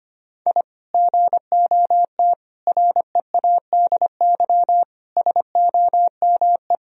Morse code, heard as a keyed tone, about 25 wpm, 700 Hz. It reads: I GOT READY SOME
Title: Sentences from Top 1000 Words - Review Part 3 - 25wpm